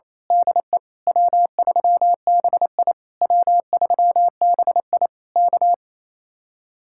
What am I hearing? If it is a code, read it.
Morse code, heard as a keyed tone, about 27 wpm, 700 Hz. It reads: DE W3BI W3BI K
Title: Speed-Racing Common QSO Phrases - 28wpm